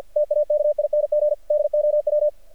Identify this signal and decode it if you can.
Morse code, heard as a keyed tone, about 49 wpm, 600 Hz. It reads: TAKING DOW
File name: archived / CRSnet